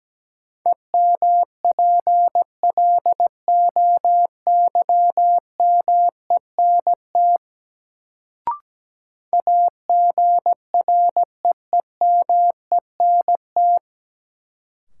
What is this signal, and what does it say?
Morse code, heard as a keyed tone, about 17 wpm, 700 Hz. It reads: EMPLOYMENT AGREEMENT
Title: Sentences from Top 1500 Words - New Word Drill - 17wpm